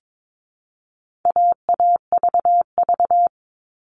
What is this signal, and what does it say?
Morse code, heard as a keyed tone, about 22 wpm, 700 Hz. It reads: AAVV
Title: All Cut Numbers - 4-Digit 22wpm